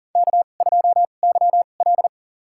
Morse code, heard as a keyed tone, about 40 wpm, spaced as for 29 wpm, 700 Hz. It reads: K2YL
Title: USA Call Signs - 4 Characters - 2x Character Spacing - 40wpm